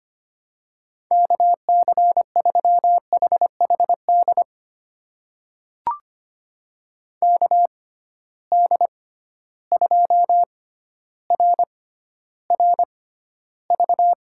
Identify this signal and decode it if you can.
Morse code, heard as a keyed tone, about 25 wpm, spaced as for 13 wpm, 700 Hz. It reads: KC3HHD KD2RRV
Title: USA Call Signs - 6 Characters - 6x Character Spacing - 25wpm